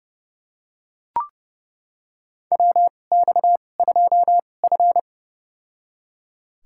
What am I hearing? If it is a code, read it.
Morse code, heard as a keyed tone, about 30 wpm, spaced as for 22 wpm, 700 Hz. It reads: WX2F
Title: USA Call Signs - 4 Characters - 2x Character Spacing - 30wpm